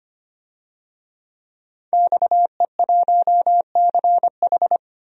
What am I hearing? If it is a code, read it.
Morse code, heard as a keyed tone, about 25 wpm, 700 Hz. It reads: XE1CH
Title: DXCC-100 Call Signs - 5 Characters - 25wpm